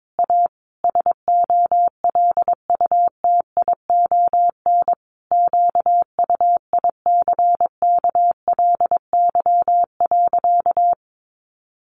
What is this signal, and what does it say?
Morse code, heard as a keyed tone, about 22 wpm, 700 Hz. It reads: A SOLUTION QUICKLY.
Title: Sentences from Top 1300 Words 22wpm